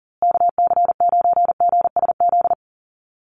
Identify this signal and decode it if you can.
Morse code, heard as a keyed tone, about 40 wpm, 700 Hz. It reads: KC9GSZ
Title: USA Call Signs Repeated 3x 40wpm